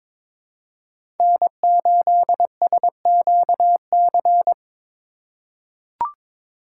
Morse code, heard as a keyed tone, about 22 wpm, 700 Hz. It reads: N8SQC
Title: USA Call Signs - 4x Character Spacing - 22wpm